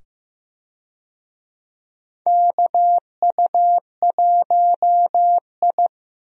Morse code, heard as a keyed tone, about 15 wpm, 700 Hz. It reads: KU1I
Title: USA Call Signs - 4 Characters - 15wpm